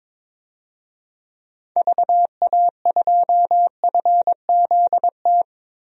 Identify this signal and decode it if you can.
Morse code, heard as a keyed tone, about 22 wpm, 700 Hz. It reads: VA2FZT
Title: DXCC-100 Call Signs - 6 Characters - 22wpm